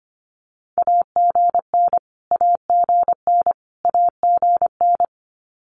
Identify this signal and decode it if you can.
Morse code, heard as a keyed tone, about 25 wpm, 700 Hz. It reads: AGN AGN AGN
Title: Abbreviations Repeated 3x - 25wpm